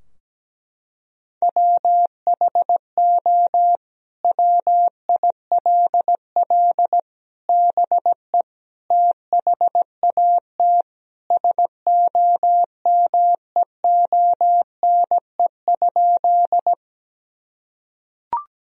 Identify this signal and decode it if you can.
Morse code, heard as a keyed tone, about 17 wpm, 700 Hz. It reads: WHO WILL BE THAT SOMEONE?
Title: Sentences from Top 1000 Words - Review Part 4 - 17wpm